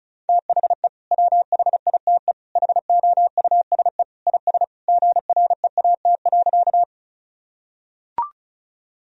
Morse code, heard as a keyed tone, about 36 wpm, 700 Hz. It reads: THE WHITE HOUSE IS GREAT.
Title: Sentences from Top 1000 Words - Review Part 1 - 35wpm